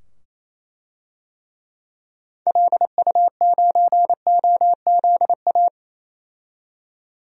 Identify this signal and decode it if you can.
Morse code, heard as a keyed tone, about 27 wpm, 700 Hz. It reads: LU9OZA
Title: DXCC-100 Call Signs - 6 Characters - 28wpm